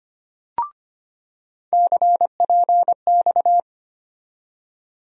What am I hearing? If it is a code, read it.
Morse code, heard as a keyed tone, about 25 wpm, 700 Hz. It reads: CPX
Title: USA Call Sign Postfixes - 3 Letters 25wpm